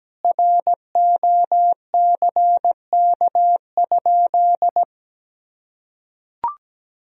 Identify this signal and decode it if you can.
Morse code, heard as a keyed tone, about 17 wpm, 700 Hz. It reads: ROCK?
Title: Sentences from Top 1000 Words - Review Part 2 - 17wpm